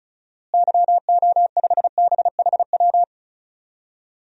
Morse code, heard as a keyed tone, about 34 wpm, 700 Hz. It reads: YO5BHW
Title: DXCC-100 Call Signs - 6 Characters - Repeated 2x - 35wpm